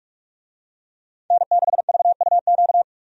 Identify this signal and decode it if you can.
Morse code, heard as a keyed tone, about 46 wpm, 700 Hz. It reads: N6VUQ
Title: USA Call Signs - 5 Characters - 45wpm